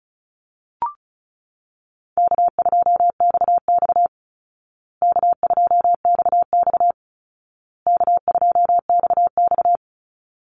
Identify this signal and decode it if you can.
Morse code, heard as a keyed tone, about 34 wpm, 700 Hz. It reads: K2XX K2XX K2XX
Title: Call Signs Repeated 2x - 35wpm